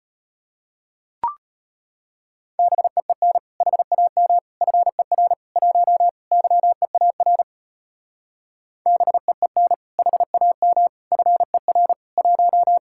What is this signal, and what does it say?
Morse code, heard as a keyed tone, about 33 wpm, 700 Hz. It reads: BEEN HAM FER 1 YEAR BEEN HAM FER 1